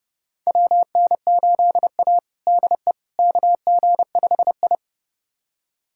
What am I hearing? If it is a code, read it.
Morse code, heard as a keyed tone, about 30 wpm, 700 Hz. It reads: WN8A DE KG5I
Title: USA Callsign DE Callsign - 4 Characters - 30wpm